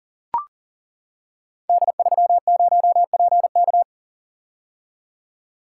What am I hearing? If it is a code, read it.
Morse code, heard as a keyed tone, about 42 wpm, 700 Hz. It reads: D30PK
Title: Code Groups - 5 Letter:Numbers - 40wpm